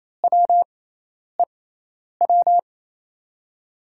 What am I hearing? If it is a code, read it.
Morse code, heard as a keyed tone, about 28 wpm, spaced as for 10 wpm, 700 Hz. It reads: WEW
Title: USA Call Signs - 5 Characters - 6x Character Spacing - 28wpm